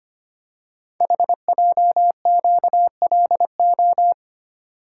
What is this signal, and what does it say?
Morse code, heard as a keyed tone, about 25 wpm, 700 Hz. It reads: HJQLO
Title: Code Groups - 5 Letters - 25wpm